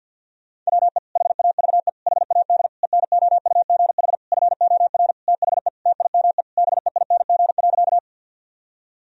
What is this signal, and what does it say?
Morse code, heard as a keyed tone, about 50 wpm, 700 Hz. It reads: WE HAVE HAD ENOUGH FOR THE TIME BEING.
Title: Sentences from Top 1000 Words - Review Part 3 - 50wpm